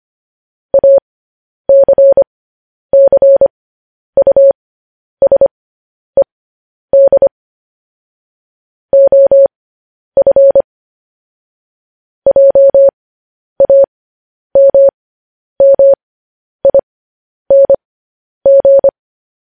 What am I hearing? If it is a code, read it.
Morse code, heard as a keyed tone, about 25 wpm, spaced as for 10 wpm, 550 Hz. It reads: ACCUSED OF JAMMING